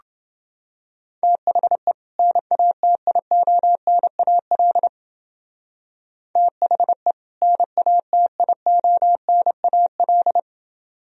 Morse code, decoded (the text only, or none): THE NATIONAL THE NATIONAL